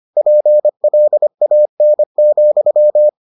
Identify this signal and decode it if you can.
Morse code, heard as a keyed tone, about 25 wpm, 600 Hz. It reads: PLAN,